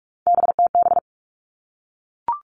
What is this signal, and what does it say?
Morse code, heard as a keyed tone, about 44 wpm, 700 Hz. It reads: BTB
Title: All Cut Numbers - 3-Digit 45wpm